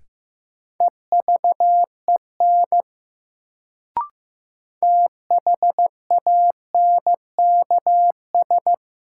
Morse code, heard as a keyed tone, about 16 wpm, 700 Hz. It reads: EVEN THANKS
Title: Sentences from Top 600 Words - New Word Drill - 15wpm